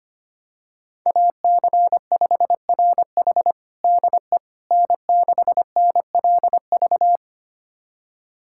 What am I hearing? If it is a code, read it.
Morse code, heard as a keyed tone, about 25 wpm, 700 Hz. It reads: AC5RH DE N6NLV
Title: USA Callsign DE Callsign 25wpm